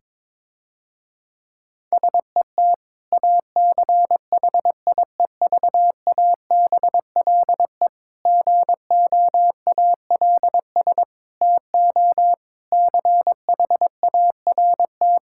Morse code, heard as a keyed tone, about 21 wpm, 700 Hz. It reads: SET ACHIEVABLE GOALS TO CHART
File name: Difficult Sentences with Hints - 22wpm